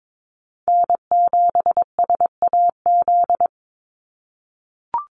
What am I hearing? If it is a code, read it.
Morse code, heard as a keyed tone, about 22 wpm, 700 Hz. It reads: N7SAZ
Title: USA Call Signs 22wpm